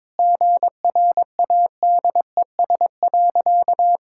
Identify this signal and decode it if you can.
Morse code, heard as a keyed tone, about 22 wpm, 700 Hz. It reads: GRADES.
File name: practice / 22wpm / Sentences with 3-Word Phrases - 22wpm